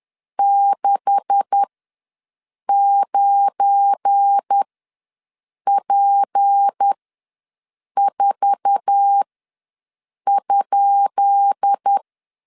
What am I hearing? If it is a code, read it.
Morse code, heard as a keyed tone, about 10 wpm, spaced as for 6 wpm, 800 Hz. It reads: 69P4?
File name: LEZIONI / 10 SIGNS / 10B INT ws 10wpm 4134